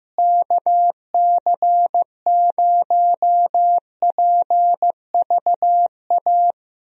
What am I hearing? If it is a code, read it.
Morse code, heard as a keyed tone, about 15 wpm, 700 Hz. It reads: KC0PVA
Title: USA Call Signs - 6 Characters - Repeated 2x - 15wpm